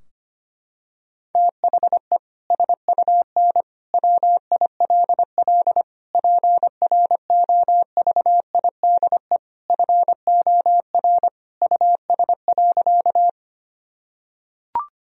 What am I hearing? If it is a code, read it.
Morse code, heard as a keyed tone, about 25 wpm, 700 Hz. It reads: THE SUN WILL PROVIDE FOR US.
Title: Sentences from Top 1000 Words - Review Part 3 - 25wpm